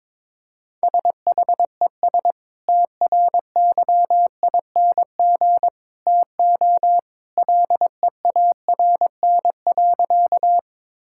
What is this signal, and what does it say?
Morse code, heard as a keyed tone, about 22 wpm, 700 Hz. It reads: SHES TRYING TO LEARN.
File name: practice / 22wpm / Sentences with 2-Word Phrases - 22wpm